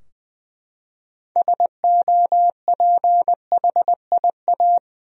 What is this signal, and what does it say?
Morse code, heard as a keyed tone, about 20 wpm, 700 Hz. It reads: SOPHIA
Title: USA Common Names - 20wpm